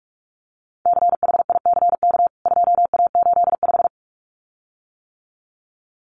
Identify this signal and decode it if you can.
Morse code, heard as a keyed tone, about 44 wpm, 700 Hz. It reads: CHICK 2A85
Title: CWT Exchange 45wpm